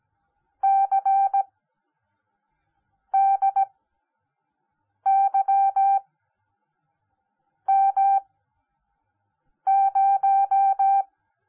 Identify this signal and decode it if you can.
Morse code, heard as a keyed tone, about 17 wpm, spaced as for 5 wpm, 780 Hz. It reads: CDYM0